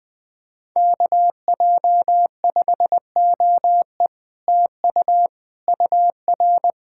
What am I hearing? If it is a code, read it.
Morse code, heard as a keyed tone, about 20 wpm, 700 Hz. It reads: KJ5OE TU UR
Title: Speed-Racing Common QSO Phrases - 17wpm